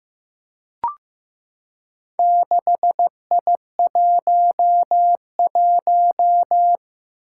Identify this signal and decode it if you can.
Morse code, heard as a keyed tone, about 15 wpm, 700 Hz. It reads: 6I11
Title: Four Challenging Letter-Number Combinations 15wpm